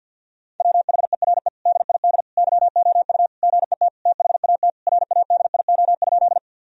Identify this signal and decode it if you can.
Morse code, heard as a keyed tone, about 47 wpm, 700 Hz. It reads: WHERE DID YOU GET THAT RADIO?
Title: Sentences from Top 1000 Words - Review Part 3 - 50wpm